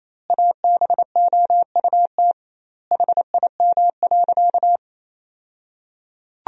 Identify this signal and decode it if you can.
Morse code, heard as a keyed tone, about 27 wpm, 700 Hz. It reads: ABOUT HIM.
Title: 2x Word-Spacing - Sentences from Top 200 Words - 28wpm